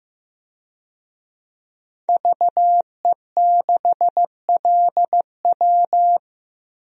Morse code, heard as a keyed tone, about 15 wpm, 700 Hz. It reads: VE6LW
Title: DXCC-100 Call Signs - 5 Characters - Repeated 3x - 15wpm